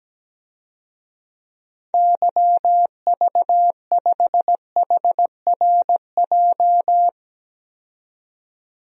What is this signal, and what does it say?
Morse code, heard as a keyed tone, about 17 wpm, 700 Hz. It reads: YV5HRJ
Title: DXCC-100 Call Signs - 6 Characters - 17wpm